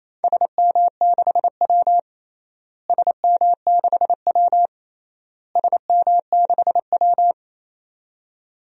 Morse code, heard as a keyed tone, about 27 wpm, 700 Hz. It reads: SM6W SM6W SM6W
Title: DXCC-100 Call Signs - 4 Characters - Repeated 3x - 28wpm